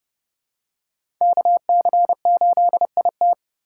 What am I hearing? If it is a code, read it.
Morse code, heard as a keyed tone, about 30 wpm, 700 Hz. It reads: KC8IT
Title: USA Call Signs - 5 Characters - 2x Character Spacing - 30wpm